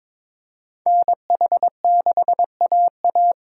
Morse code, heard as a keyed tone, about 26 wpm, 700 Hz. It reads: NH6AA
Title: Speed Racing - USA 2x2 Call Signs 22wpm